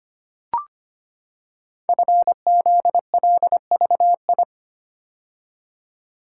Morse code, heard as a keyed tone, about 25 wpm, 700 Hz. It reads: FZLVI
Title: Code Groups - 5 Letters - 25wpm